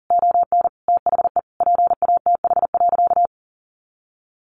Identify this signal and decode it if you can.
Morse code, heard as a keyed tone, about 40 wpm, 700 Hz. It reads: ON THE PATH.
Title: A Sound Of Thunder 40wpm